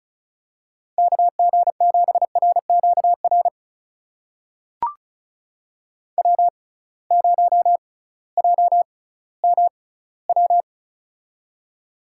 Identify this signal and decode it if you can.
Morse code, heard as a keyed tone, about 36 wpm, spaced as for 19 wpm, 700 Hz. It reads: KG7RQR W0JMW
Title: USA Call Signs - 6x Character Spacing - 35wpm